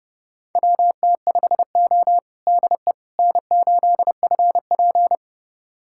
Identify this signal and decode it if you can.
Morse code, heard as a keyed tone, about 30 wpm, 700 Hz. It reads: WT5O DE N8FP
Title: USA Callsign DE Callsign - 4 Characters - 30wpm